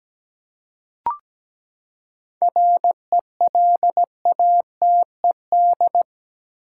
Morse code, heard as a keyed tone, about 17 wpm, 700 Hz. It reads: RELATED
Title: Seven Letter Words - 17wpm